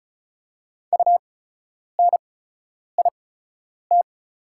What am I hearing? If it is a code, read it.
Morse code, heard as a keyed tone, about 36 wpm, spaced as for 10 wpm, 700 Hz. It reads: UNIT
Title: 8x Character Spacing - Top 500 Words - 35wpm